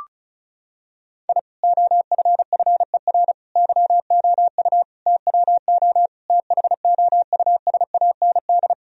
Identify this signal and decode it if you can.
Morse code, heard as a keyed tone, about 36 wpm, 700 Hz. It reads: I OFFER YOU TWO THOUSAND
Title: Sentences from Top 1000 Words - Review Part 4 - 35wpm